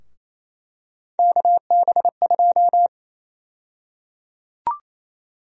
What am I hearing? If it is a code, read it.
Morse code, heard as a keyed tone, about 28 wpm, 700 Hz. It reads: KB2
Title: USA Call Sign Prefixes - 28wpm